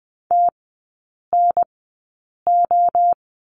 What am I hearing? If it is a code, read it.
Morse code, heard as a keyed tone, about 20 wpm, spaced as for 8 wpm, 700 Hz. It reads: TNO
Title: 035 Lesson 7e - Letter s - Three Characters 20wpm